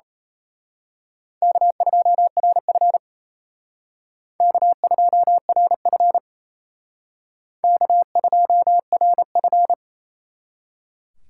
Morse code, heard as a keyed tone, about 32 wpm, 700 Hz. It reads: K2RF K2RF K2RF
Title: Speed Racing - USA 1x2 Call Signs 28wpm